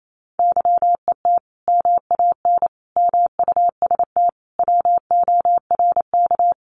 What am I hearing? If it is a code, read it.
Morse code, heard as a keyed tone, about 28 wpm, 700 Hz. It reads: YET MAN MUST WORK
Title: Sentences from Top 1500 Words 28wpm